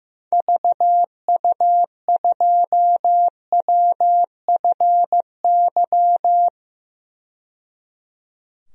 Morse code, heard as a keyed tone, about 15 wpm, 700 Hz. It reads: VU2WFY
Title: DXCC-100 Call Signs - 6 Characters - 15wpm